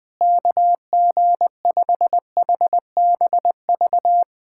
Morse code, heard as a keyed tone, about 19 wpm, 700 Hz. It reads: KG5HBV
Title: USA Callsign DE Callsign 20wpm